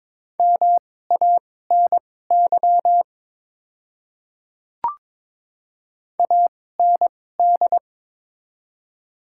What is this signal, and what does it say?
Morse code, heard as a keyed tone, about 21 wpm, spaced as for 16 wpm, 700 Hz. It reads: MANY AND
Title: 2x Character Spacing - Top 200 Words - 22wpm